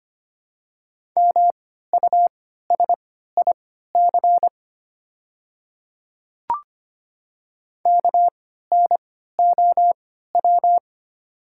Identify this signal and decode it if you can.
Morse code, heard as a keyed tone, about 25 wpm, spaced as for 14 wpm, 700 Hz. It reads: MUSIC KNOW
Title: 3x Character Spacing - Top 300 Words - 25wpm